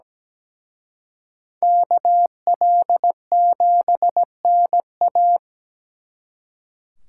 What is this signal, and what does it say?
Morse code, heard as a keyed tone, about 20 wpm, 700 Hz. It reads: KL7NA
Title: Speed Racing - USA 2x2 Call Signs 17wpm